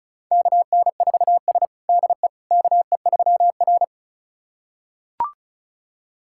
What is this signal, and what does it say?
Morse code, heard as a keyed tone, about 36 wpm, 700 Hz. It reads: KN4S DE KE3R
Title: USA Callsign DE Callsign - 4 Characters - 35wpm